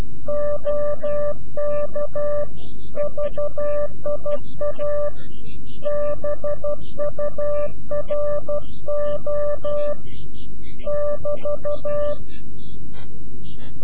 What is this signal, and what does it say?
Morse code, heard as a keyed tone, about 12 wpm, 590 Hz. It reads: OK VIA BURO <BT>
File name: CW-QSO / CQ Call / CQ-QSO